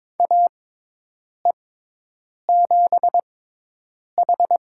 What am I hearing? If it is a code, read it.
Morse code, heard as a keyed tone, about 22 wpm, spaced as for 8 wpm, 700 Hz. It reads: AE7H
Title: USA Call Signs - 5 Characters - 6x Character Spacing - 22wpm